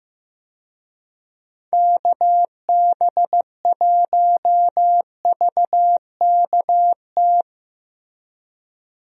Morse code, heard as a keyed tone, about 15 wpm, 700 Hz. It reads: KB1VKT
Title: USA Call Signs - 6 Characters - Repeated 3x - 15wpm